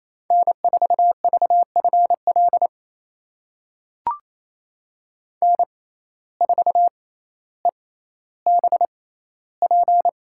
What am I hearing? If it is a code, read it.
Morse code, heard as a keyed tone, about 27 wpm, spaced as for 14 wpm, 700 Hz. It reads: N4VFL N4EBP
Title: USA Call Signs - 5 Characters - 6x Character Spacing - 28wpm